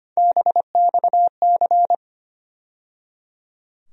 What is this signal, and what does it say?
Morse code, heard as a keyed tone, about 25 wpm, 700 Hz. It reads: BXC
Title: USA Call Sign Postfixes - 3 Letters 25wpm